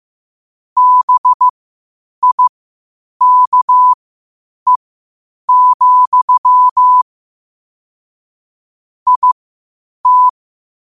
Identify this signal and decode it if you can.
Morse code, heard as a keyed tone, about 15 wpm, spaced as for 8 wpm, 1000 Hz. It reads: BIKE, IT
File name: qotdfiles / 8wpm